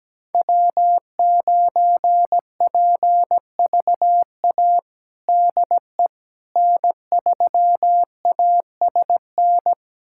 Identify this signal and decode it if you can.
Morse code, heard as a keyed tone, about 17 wpm, 700 Hz. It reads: W9PVA DE N3ASN
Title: USA Callsign DE Callsign - 5 Characters - 17wpm